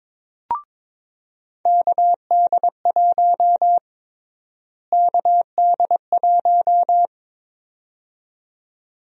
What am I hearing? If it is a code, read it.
Morse code, heard as a keyed tone, about 22 wpm, 700 Hz. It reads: KD1 KD1
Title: USA Call Sign Prefixes - Repeated 2x - 22wpm